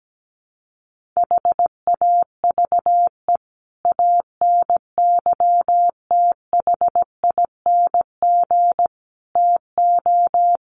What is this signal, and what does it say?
Morse code, heard as a keyed tone, about 17 wpm, 700 Hz. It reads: HAVE ANYTHING TO
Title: Sentences from Top 1500 Words 17wpm